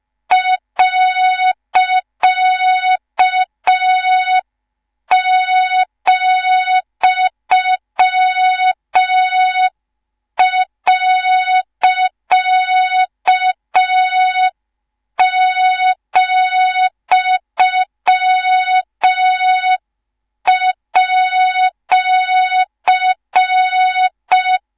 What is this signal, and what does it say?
Morse code, heard as a keyed tone, about 5 wpm, 3700 Hz, with noise. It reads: .,.,@